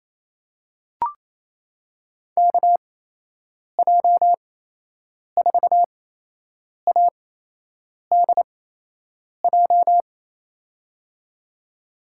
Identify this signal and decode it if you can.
Morse code, heard as a keyed tone, about 27 wpm, spaced as for 8 wpm, 700 Hz. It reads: KJ4ADJ
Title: USA Call Signs - 6 Characters - 8x Character Spacing - 28wpm